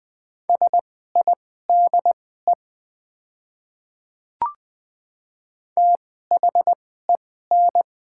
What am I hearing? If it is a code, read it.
Morse code, heard as a keyed tone, about 20 wpm, spaced as for 14 wpm, 700 Hz. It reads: SIDE THEN
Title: 2x Character Spacing - Top 100 Words - 20wpm